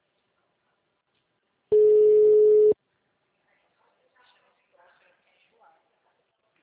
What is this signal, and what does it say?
Ringback tone (Europe, 425 Hz), with speech.